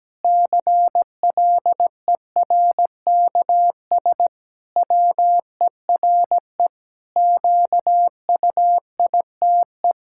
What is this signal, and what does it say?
Morse code, heard as a keyed tone, about 17 wpm, 700 Hz. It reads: CLERKS WERE QUITE
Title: Pangram Challenge v2 - 17wpm